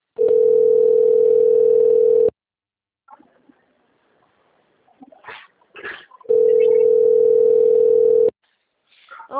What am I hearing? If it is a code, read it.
Ringback tone (North America), 440 + 480 Hz.